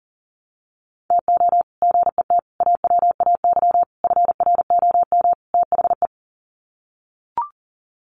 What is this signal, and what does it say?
Morse code, heard as a keyed tone, about 40 wpm, 700 Hz. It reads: TO GET AWAY FROM THE
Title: Sets of 5 Words Encore2 40wpm